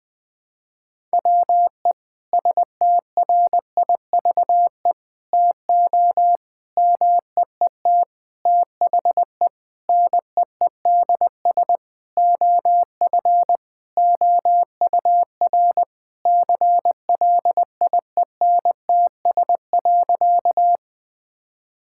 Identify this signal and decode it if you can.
Morse code, heard as a keyed tone, about 21 wpm, 700 Hz. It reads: WE STRIVE TO MEET THE NEEDS OF OUR CLIENTS.
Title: Sentences with 5-Word Phrases - 20wpm